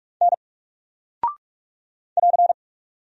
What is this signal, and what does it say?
Morse code, heard as a keyed tone, about 44 wpm, 700 Hz. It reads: N <AR>
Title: Single Character (Letter:Number:Punctuation:Prosign) Alt 45wpm